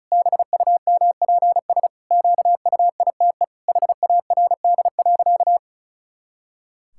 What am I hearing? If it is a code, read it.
Morse code, heard as a keyed tone, about 36 wpm, 700 Hz. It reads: BUMPS QUITE HARD.
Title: Pangram Challenge v2 - 35wpm